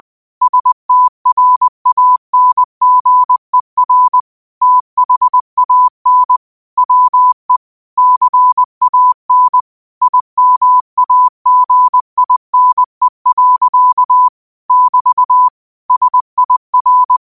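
Morse code, heard as a keyed tone, about 20 wpm, 1000 Hz. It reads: STRANGER THAN WE CAN IMAGINE. <BT> SIR
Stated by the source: Morse code practice at 20 wpm: famous quotations sent in telegraphy, from URVAG.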